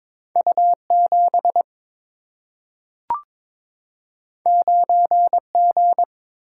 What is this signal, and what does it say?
Morse code, heard as a keyed tone, about 22 wpm, 700 Hz. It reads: U 7 9 G
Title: Single-Letter Single-Number 22wpm